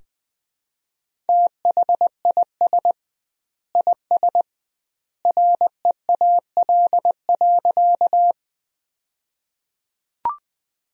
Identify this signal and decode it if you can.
Morse code, heard as a keyed tone, about 21 wpm, 700 Hz. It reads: THIS IS REAL.
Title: 2x Word-Spacing - Sentences from Top 300 Words - 20wpm